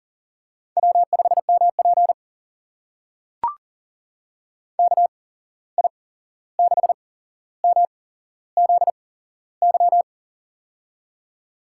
Morse code, heard as a keyed tone, about 39 wpm, spaced as for 15 wpm, 700 Hz. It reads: W5MP KI6MZY
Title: USA Call Signs - 8x Character Spacing - 40wpm